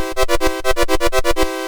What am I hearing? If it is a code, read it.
There is no code.